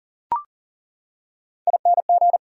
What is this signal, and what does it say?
Morse code, heard as a keyed tone, about 40 wpm, 700 Hz. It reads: ING
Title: Prefixes and Suffixes 40wpm